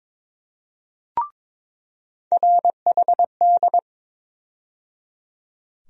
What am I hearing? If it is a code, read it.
Morse code, heard as a keyed tone, about 22 wpm, 700 Hz. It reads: RHD
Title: USA Call Sign Postfixes - 3 Letters 22wpm